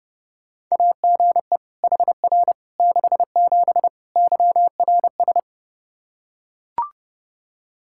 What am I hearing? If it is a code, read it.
Morse code, heard as a keyed tone, about 30 wpm, 700 Hz. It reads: AGE HR 67 YRS
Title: Practice QSO - Exchange 3 - More Info - 30wpm